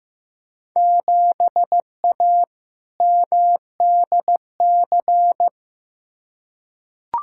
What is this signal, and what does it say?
Morse code, heard as a keyed tone, about 15 wpm, 700 Hz. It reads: 7A MDC
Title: Field Day Exchanges Encore - 15wpm